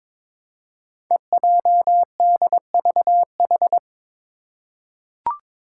Morse code, heard as a keyed tone, about 22 wpm, 700 Hz. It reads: EJDVH
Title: Five Challenging Letter-Number Combinations 22wpm